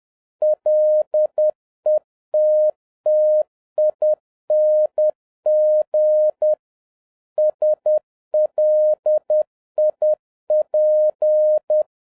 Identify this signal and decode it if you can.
Morse code, heard as a keyed tone, about 10 wpm, 610 Hz. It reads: LETTING SLIP
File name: 2011-07-mp3 / n56 BACK FROM A BIG ACCIDENT 10wpm